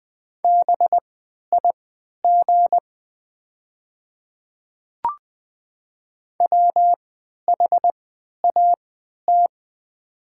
Morse code, heard as a keyed tone, about 20 wpm, spaced as for 11 wpm, 690 Hz. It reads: BIG WHAT
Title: 3x Character Spacing - Top 300 Words - 20wpm